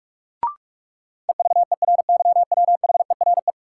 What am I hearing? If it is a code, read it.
Morse code, heard as a keyed tone, about 42 wpm, 700 Hz. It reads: EVERYWHERE
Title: Sentences from Top 1600 Words - New Word Drill - 45wpm